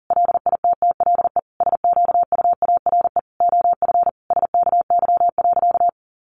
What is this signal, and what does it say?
Morse code, heard as a keyed tone, about 40 wpm, 700 Hz. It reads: LITTLE SQUARE OF SKY.
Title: Sentences from Top 1200 Words 40wpm